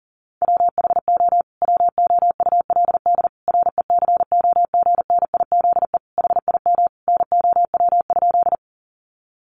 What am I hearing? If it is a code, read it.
Morse code, heard as a keyed tone, about 40 wpm, 700 Hz. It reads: WHO WOULD RECOGNIZE HIM NOW?
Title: Sentences from Top 1400 Words 40wpm